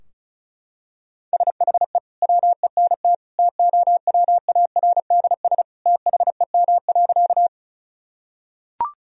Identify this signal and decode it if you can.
Morse code, heard as a keyed tone, about 36 wpm, 700 Hz. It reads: SHE WENT TOWARDS THEM.
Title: Sentences from Top 1000 Words - Review Part 2 - 35wpm